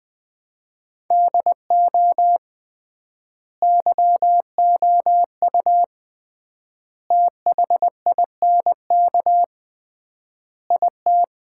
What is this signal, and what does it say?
Morse code, heard as a keyed tone, about 21 wpm, 700 Hz. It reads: DO YOU THINK IT
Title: 3x Word-Spacing - Sets of 4 Words - 20wpm